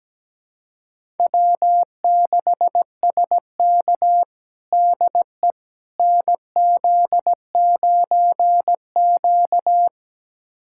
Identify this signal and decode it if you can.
Morse code, heard as a keyed tone, about 17 wpm, 700 Hz. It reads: W6SK DE NZ9Q
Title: USA Callsign DE Callsign - 4 Characters - 17wpm